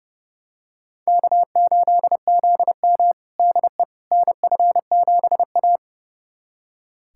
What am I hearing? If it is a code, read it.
Morse code, heard as a keyed tone, about 32 wpm, 700 Hz. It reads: K8ZM DE NF7A
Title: USA Callsign DE Callsign - 4 Characters - 30wpm